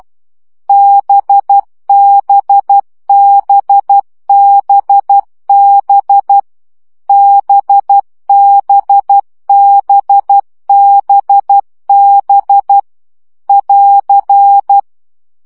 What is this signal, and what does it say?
Morse code, heard as a keyed tone, about 12 wpm, 800 Hz. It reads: BBBBB BBBBB <AR>
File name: CW gyakorlas b 10 baud